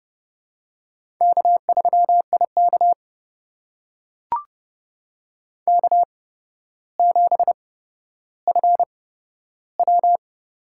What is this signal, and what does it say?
Morse code, heard as a keyed tone, about 30 wpm, spaced as for 13 wpm, 700 Hz. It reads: K3IK K7FW
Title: USA Call Signs - 4 Characters - 8x Character Spacing - 30wpm